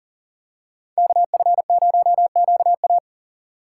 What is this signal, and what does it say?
Morse code, heard as a keyed tone, about 38 wpm, 700 Hz. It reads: KF0QA
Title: USA Call Signs - 5 Characters - 4x Character Spacing - 40wpm